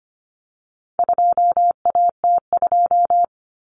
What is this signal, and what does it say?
Morse code, heard as a keyed tone, about 25 wpm, 700 Hz. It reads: 2AT2
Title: Common Cut Numbers - 4-Digit 25wpm